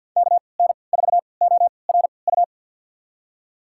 Morse code, heard as a keyed tone, about 52 wpm, spaced as for 29 wpm, 700 Hz. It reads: KN4ORU
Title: USA Call Signs - 6 Characters - 3x Character Spacing - 50wpm